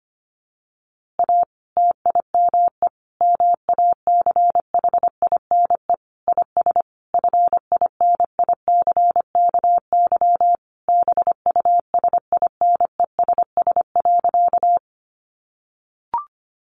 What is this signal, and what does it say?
Morse code, heard as a keyed tone, about 25 wpm, 700 Hz. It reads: A TIME MACHINE IS FINICKY BUSINESS.
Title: A Sound Of Thunder 25wpm